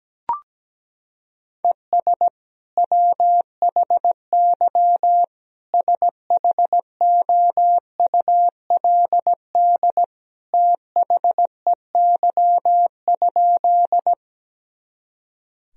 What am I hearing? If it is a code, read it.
Morse code, heard as a keyed tone, about 17 wpm, 700 Hz. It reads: ES WHY SHOULD THEY?
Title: Sentences from Top 300 Words and Abbreviations 17wpm